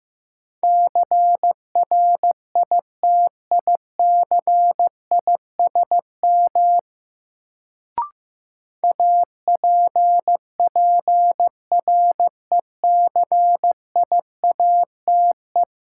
Morse code, heard as a keyed tone, about 15 wpm, 700 Hz. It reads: CRITICISM APPRECIATE
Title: Sentences from Top 1300 Words - New Word Drill - 15wpm